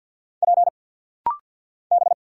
Morse code, heard as a keyed tone, about 50 wpm, 700 Hz. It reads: P B
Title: Single Letters - Rapid-Fire 50wpm